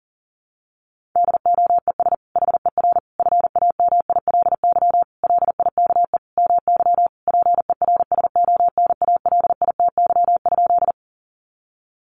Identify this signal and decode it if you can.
Morse code, heard as a keyed tone, about 40 wpm, 700 Hz. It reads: DOES HER FAMILY LIKE MY PERSONALITY?
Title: Sentences from Top 1400 Words 40wpm